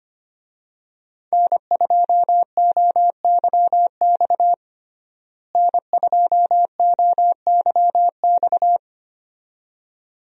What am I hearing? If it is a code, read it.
Morse code, heard as a keyed tone, about 25 wpm, 700 Hz. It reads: N2OYX N2OYX
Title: USA Call Signs - 5 Characters - Repeated 2x - 25wpm